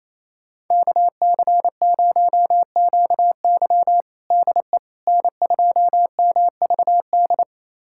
Morse code, heard as a keyed tone, about 28 wpm, 700 Hz. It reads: KC0QY DE N2MVD
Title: USA Callsign DE Callsign - 5 Characters - 28wpm